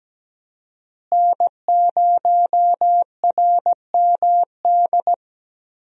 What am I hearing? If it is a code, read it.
Morse code, heard as a keyed tone, about 17 wpm, 700 Hz. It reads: N0RMD
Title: USA Call Signs - 5 Characters - Repeated 3x - 17wpm